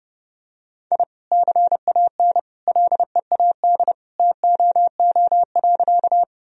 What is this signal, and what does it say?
Morse code, heard as a keyed tone, about 30 wpm, 700 Hz. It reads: I CAN LEAD TOO.
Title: Sentences from Top 1000 Words - Review Part 2 - 30wpm